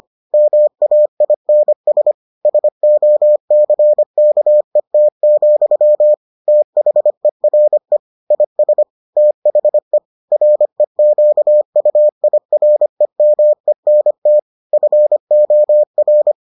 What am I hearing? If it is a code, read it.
Morse code, heard as a keyed tone, about 25 wpm, 600 Hz. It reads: MAINS SOCKET, THERE IS THE REQUIREMENT FOR